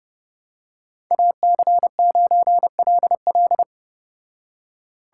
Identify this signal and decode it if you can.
Morse code, heard as a keyed tone, about 30 wpm, 700 Hz. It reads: AC9LL